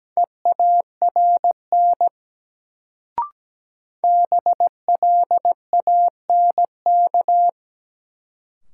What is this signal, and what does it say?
Morse code, heard as a keyed tone, about 17 wpm, 700 Hz. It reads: EARN BLANK
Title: Sentences from Top 1500 Words - New Word Drill Alt - 17wpm